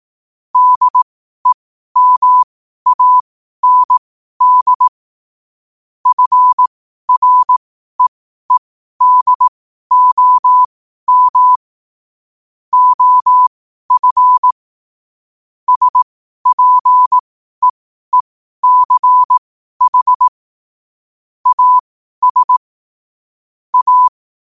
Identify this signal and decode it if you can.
Morse code, heard as a keyed tone, about 18 wpm, spaced as for 12 wpm, 1000 Hz. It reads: DEMAND FREEDOM OF SPEECH AS A
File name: qotdfiles / 12wpm